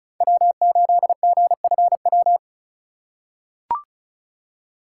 Morse code, heard as a keyed tone, about 36 wpm, 700 Hz. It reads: W8GFW
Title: USA Call Signs - 5 Characters - 4x Character Spacing - 35wpm